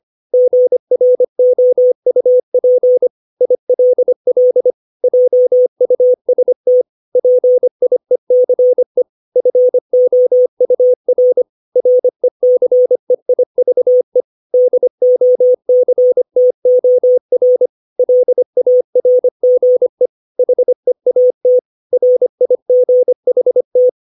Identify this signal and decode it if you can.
Morse code, heard as a keyed tone, about 25 wpm, 500 Hz. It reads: GROUP ILL JUST PIECE FOUR RECEIVE DOCTOR LARGE HEAT RIGHT
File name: 632 words / 632 Words 25WPM